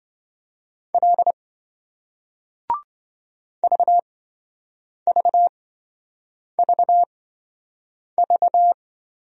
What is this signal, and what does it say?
Morse code, heard as a keyed tone, about 28 wpm, spaced as for 7 wpm, 700 Hz. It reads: L VVVV